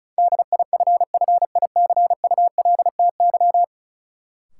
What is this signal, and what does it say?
Morse code, heard as a keyed tone, about 36 wpm, 700 Hz. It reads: DIFFICULTY
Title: Sentences from Top 1500 Words - New Word Drill - 35wpm